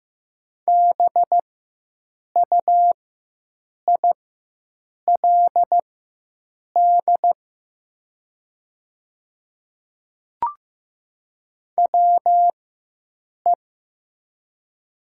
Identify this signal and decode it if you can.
Morse code, heard as a keyed tone, about 15 wpm, spaced as for 7 wpm, 700 Hz. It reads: BUILD WE
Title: 4x Character Spacing - Top 200 Words - 15wpm